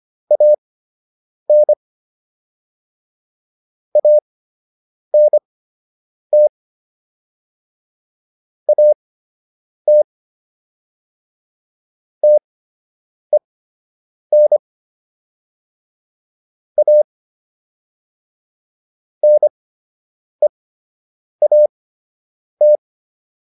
Morse code, heard as a keyed tone, about 25 wpm, spaced as for 8 wpm, 600 Hz. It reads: AN ANT AT TEN A NEAT